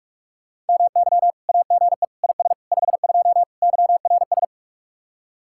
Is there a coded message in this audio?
Yes: Morse code from 0.7 to 4.4 s, heard as a keyed tone, about 58 wpm, 700 Hz. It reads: MY AGE IS 52 YRS